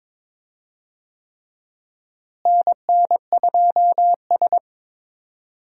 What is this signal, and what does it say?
Morse code, heard as a keyed tone, about 22 wpm, 700 Hz. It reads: NN2S
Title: USA Call Signs - 4 Characters - Repeated 3x - 22wpm